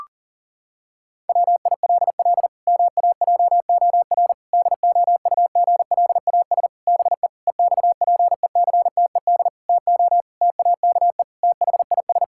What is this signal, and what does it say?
Morse code, heard as a keyed tone, about 37 wpm, 700 Hz. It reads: WILL MAJOR DOUGLAS BE EXPECTED TO TAKE THIS
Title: Pangram Challenge v2 - 40wpm